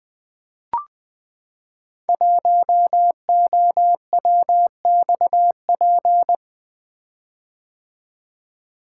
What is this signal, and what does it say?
Morse code, heard as a keyed tone, about 20 wpm, 700 Hz. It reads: 1OWXP
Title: Code Groups - 5 Letter:Numbers - 20wpm